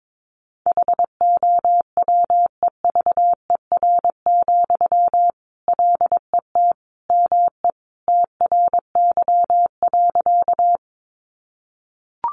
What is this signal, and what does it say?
Morse code, heard as a keyed tone, about 22 wpm, 700 Hz. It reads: HOWEVER, LET ME TRY.
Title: Sentences from Top 600 Words 22wpm